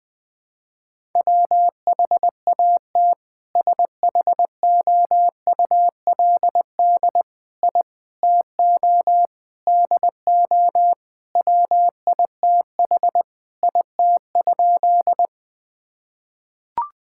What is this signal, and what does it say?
Morse code, heard as a keyed tone, about 20 wpm, 700 Hz. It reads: WHAT SHOULD I TO DO WITH IT?
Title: Sentences with 4-Word Phrases - 20wpm